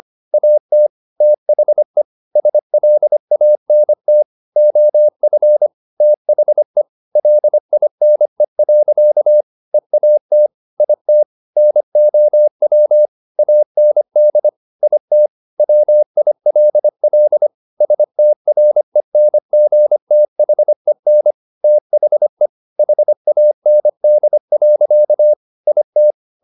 Morse code, heard as a keyed tone, about 25 wpm, 600 Hz. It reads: AT THE SLANT OF THE LINE. EAT IT NOW AND IT WILL STRENGTHEN THE HAND. IT
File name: the old man and the sea part 1 25wpm